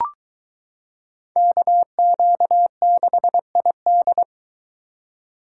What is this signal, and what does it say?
Morse code, heard as a keyed tone, about 23 wpm, 700 Hz. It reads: KQ6ID